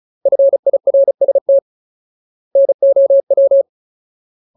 Morse code, heard as a keyed tone, about 34 wpm, 550 Hz. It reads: FIRST NOW
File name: Ultimate Word List / 35WPM Ultimate